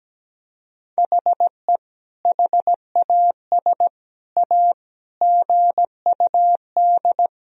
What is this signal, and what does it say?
Morse code, heard as a keyed tone, about 17 wpm, 700 Hz. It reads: HE HAS A GUD
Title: Sentences from Top 500 Words and Abbreviations 17wpm